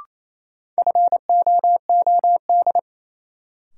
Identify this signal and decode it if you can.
Morse code, heard as a keyed tone, about 28 wpm, 700 Hz. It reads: FOOD
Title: Sets of 5 Words - New Word Drill - 28wpm